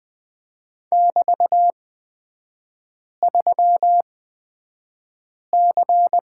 Morse code, heard as a keyed tone, about 20 wpm, spaced as for 5 wpm, 700 Hz. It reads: <BT>3C
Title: 224 Lesson 36e - BT - Three Characters 20wpm